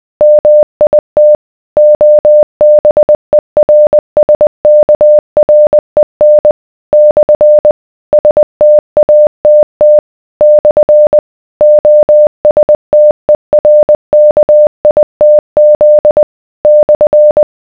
Morse code, heard as a keyed tone, about 20 wpm, 600 Hz. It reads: MIT OBERSKREN / STATT / OSTERKITZ /